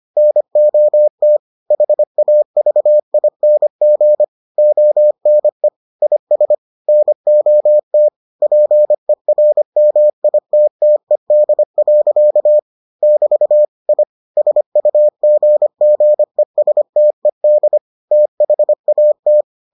Morse code, heard as a keyed tone, about 25 wpm, 600 Hz. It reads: NOT HAVING ONE IS NOT PERMITTED. <BT> I SUGGESTED THAT